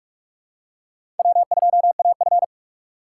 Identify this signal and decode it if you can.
Morse code, heard as a keyed tone, about 46 wpm, 700 Hz. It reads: W2AF
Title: USA Call Signs - 4 Characters - 3x Character Spacing - 45wpm